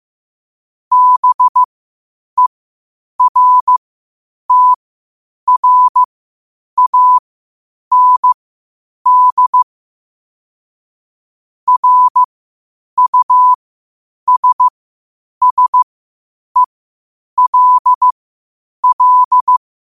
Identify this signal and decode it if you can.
Morse code, heard as a keyed tone, about 15 wpm, spaced as for 8 wpm, 1000 Hz. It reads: BERTRAND RUSSELL
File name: qotdfiles / 8wpm